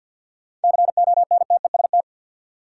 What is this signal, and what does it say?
Morse code, heard as a keyed tone, about 52 wpm, 700 Hz. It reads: CONTEST
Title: Sentences from Top 1600 Words - New Word Drill - 50wpm